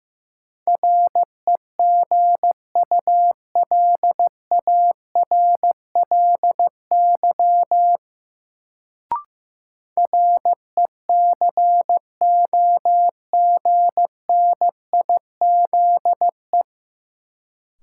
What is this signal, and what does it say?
Morse code, heard as a keyed tone, about 15 wpm, 700 Hz. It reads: REGULARLY RECOGNIZE
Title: Sentences from Top 1400 Words - New Word Drill - 15wpm